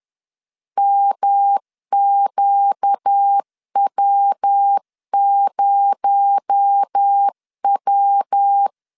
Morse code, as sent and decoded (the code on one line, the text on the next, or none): -- --.- .-- ----- .--
MQW0W